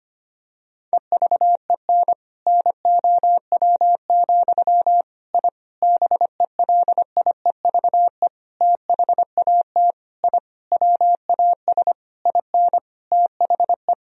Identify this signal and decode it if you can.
Morse code, heard as a keyed tone, about 25 wpm, 700 Hz. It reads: EVEN NOW, I BELIEVE THAT I WAS IN THE
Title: Sentences from Top 1000 Words - Review Part 1 - 25wpm